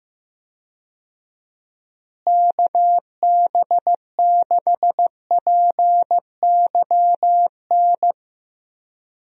Morse code, heard as a keyed tone, about 15 wpm, 700 Hz. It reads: KB6PYN
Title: USA Call Signs - 6 Characters - Repeated 2x - 15wpm